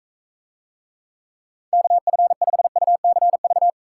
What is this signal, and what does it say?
Morse code, heard as a keyed tone, about 43 wpm, 700 Hz. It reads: KF5UCV